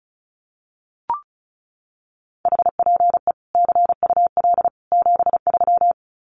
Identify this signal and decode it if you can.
Morse code, heard as a keyed tone, about 34 wpm, 700 Hz. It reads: HPE CUL 73
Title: Common QSO Phrases 35wpm